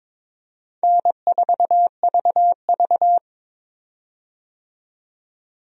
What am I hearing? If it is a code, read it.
Morse code, heard as a keyed tone, about 22 wpm, 700 Hz. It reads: N4VV
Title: USA Call Signs - 4 Characters - 22wpm